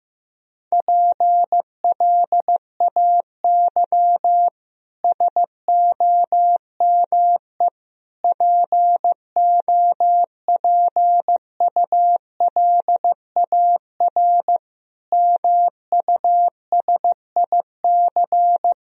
Morse code, heard as a keyed tone, about 16 wpm, 700 Hz. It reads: PLAY SOME POPULAR MUSIC
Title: Sentences from Top 1000 Words - Review Part 3 - 15wpm